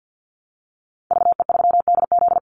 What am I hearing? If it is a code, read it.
Morse code, heard as a keyed tone, about 50 wpm, 700 Hz. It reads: VE3DZ
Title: Call Signs Repeated 3x - 50wpm